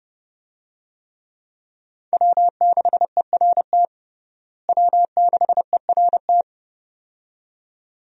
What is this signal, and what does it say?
Morse code, heard as a keyed tone, about 30 wpm, 700 Hz. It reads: W6ERT W6ERT
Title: USA Call Signs - 5 Characters - Repeated 2x - 30wpm